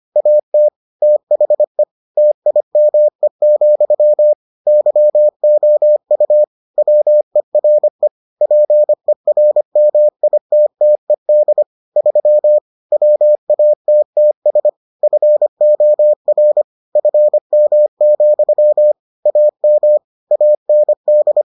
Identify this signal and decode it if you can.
Morse code, heard as a keyed tone, about 25 wpm, 600 Hz. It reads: AT THE TIME, YOU WERE PERMITTED 3 WATTS FOR FM, AM AND